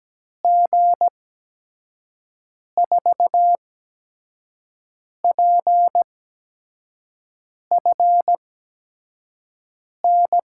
Morse code, heard as a keyed tone, about 17 wpm, 700 Hz. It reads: G 4 P F N
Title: USA Call Signs - 6 Characters - 8x Character Spacing - 17wpm